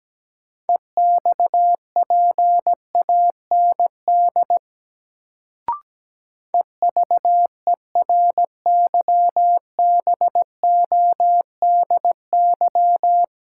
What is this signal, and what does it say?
Morse code, heard as a keyed tone, about 17 wpm, 700 Hz. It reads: EXPAND EVERYBODY
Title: Sentences from Top 1400 Words - New Word Drill Alt - 17wpm